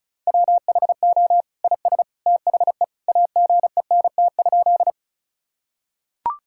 Morse code, heard as a keyed tone, about 36 wpm, 700 Hz. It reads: WHO IS THE AGENT?
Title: Sentences from Top 1600 Words 35wpm